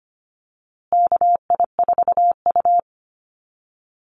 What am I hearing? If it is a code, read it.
Morse code, heard as a keyed tone, about 25 wpm, 700 Hz. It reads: KI4U
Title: USA Call Signs Repeated 3x 25wpm